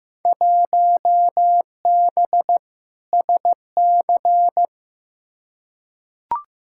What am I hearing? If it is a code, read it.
Morse code, heard as a keyed tone, about 15 wpm, 700 Hz. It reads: 1B SC
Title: Field Day Exchanges Encore - 15wpm